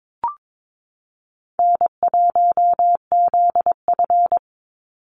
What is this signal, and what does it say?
Morse code, heard as a keyed tone, about 22 wpm, 700 Hz. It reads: N1ZF
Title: NA SOTA Chaser Callsigns 22wpm